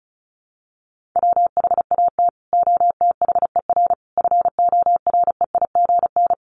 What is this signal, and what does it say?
Morse code, heard as a keyed tone, about 34 wpm, 700 Hz. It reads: WHAT OTHER FOREIGN
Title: Sentences from Top 1400 Words 35wpm